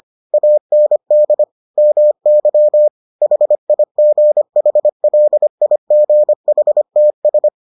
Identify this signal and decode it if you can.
Morse code, heard as a keyed tone, about 25 wpm, 600 Hz. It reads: AND MY HIGHLIGHTS